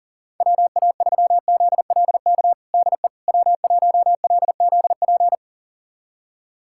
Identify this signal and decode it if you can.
Morse code, heard as a keyed tone, about 40 wpm, 700 Hz. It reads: WA3ZLK DE W1LZP
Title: USA Callsign DE Callsign 40wpm